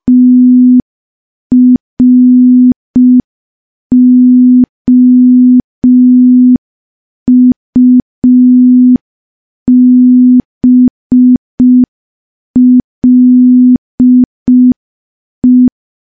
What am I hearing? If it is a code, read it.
Morse code, heard as a keyed tone, about 5 wpm, 250 Hz. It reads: TROUBLE
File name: morse / morse-training-05WPM